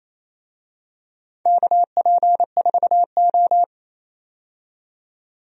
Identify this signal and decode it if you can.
Morse code, heard as a keyed tone, about 27 wpm, 700 Hz. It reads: KP4O
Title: DXCC-100 Call Signs - 4 Characters - 28wpm